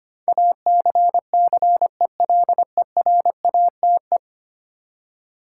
Morse code, heard as a keyed tone, about 25 wpm, 700 Hz. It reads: ACCELERATE
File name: Ten Letter Words - 25wpm